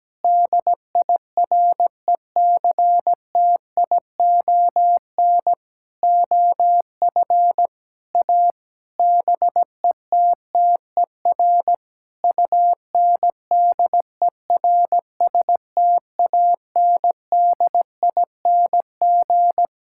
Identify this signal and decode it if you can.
Morse code, heard as a keyed tone, about 17 wpm, 700 Hz. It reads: DIRECTION OF A BETTER UNDERSTANDING
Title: Sentences from Top 1000 Words - Review Part 5 - 17wpm